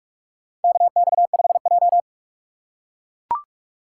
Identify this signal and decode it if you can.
Morse code, heard as a keyed tone, about 44 wpm, 690 Hz. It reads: KX5J
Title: USA Call Signs - 4 Characters - 2x Character Spacing - 45wpm